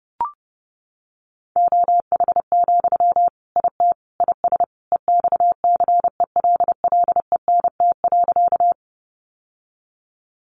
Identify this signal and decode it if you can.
Morse code, heard as a keyed tone, about 30 wpm, 700 Hz. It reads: OH, IT IS EXCELLENT.
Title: Sentences from Top 1000 Words 30wpm